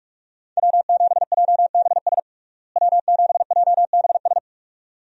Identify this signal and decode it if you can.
Morse code, heard as a keyed tone, about 46 wpm, 700 Hz. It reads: W7JBS W7JBS
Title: USA Call Signs - 5 Characters - Repeated 2x - 45wpm